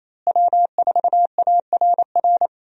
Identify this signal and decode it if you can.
Morse code, heard as a keyed tone, about 28 wpm, 700 Hz. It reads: W4ARR
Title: USA Call Signs - 5 Characters - Repeated 3x - 28wpm